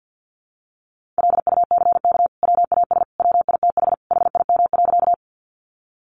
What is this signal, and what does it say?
Morse code, heard as a keyed tone, about 50 wpm, 700 Hz. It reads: LUCK WAS WITH HIM.
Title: Sentences from Top 1500 Words 50wpm